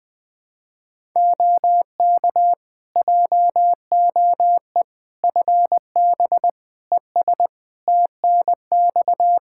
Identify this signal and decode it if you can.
Morse code, heard as a keyed tone, about 20 wpm, 700 Hz. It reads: OK JOE FB ES TNX
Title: K1USN SST QSO Practice 20wpm